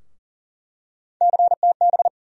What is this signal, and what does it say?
Morse code, heard as a keyed tone, about 40 wpm, 700 Hz. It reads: CTB
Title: USA Call Sign Postfixes - 3 Letters 40wpm